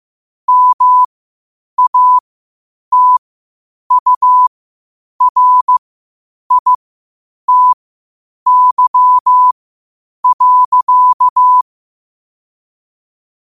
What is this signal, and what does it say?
Morse code, heard as a keyed tone, about 15 wpm, spaced as for 8 wpm, 1000 Hz. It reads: MATURITY.
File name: qotdfiles / 8wpm